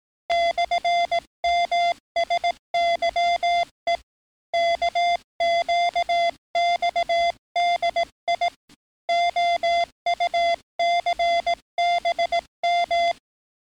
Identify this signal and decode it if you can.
Morse code, heard as a keyed tone, about 18 wpm, 690 Hz. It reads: /MSYE KQXDI OUCBM
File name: training / NAVPERS 11049to11060 / NAVPERS 11057 RA RB